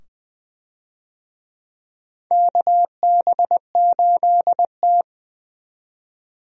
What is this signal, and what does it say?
Morse code, heard as a keyed tone, about 20 wpm, 700 Hz. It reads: KB8T
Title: USA Call Signs - 4 Characters - 20wpm